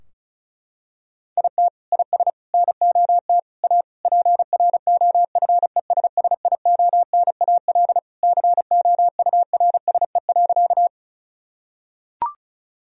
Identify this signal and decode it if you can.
Morse code, heard as a keyed tone, about 36 wpm, 700 Hz. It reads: IT IS NOT A PROFESSIONAL COURSE.
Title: Sentences from Top 1000 Words - Review Part 4 - 35wpm